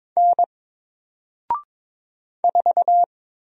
Morse code, heard as a keyed tone, about 22 wpm, 700 Hz. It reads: N 4
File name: Single-Letter-Number-22wpm